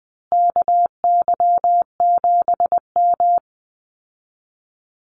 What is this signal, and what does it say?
Morse code, heard as a keyed tone, about 20 wpm, 700 Hz. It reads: KY7M
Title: Call Signs Repeated 3x - 20wpm